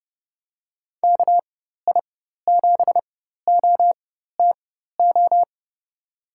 Morse code, heard as a keyed tone, about 31 wpm, spaced as for 14 wpm, 700 Hz. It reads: KI7OTO
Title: USA Call Signs - 6 Characters - 4x Character Spacing - 30wpm